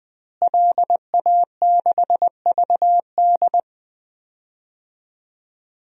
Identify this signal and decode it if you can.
Morse code, heard as a keyed tone, about 21 wpm, 700 Hz. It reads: LA6VD
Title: DXCC-100 Call Signs - 5 Characters - Repeated 3x - 20wpm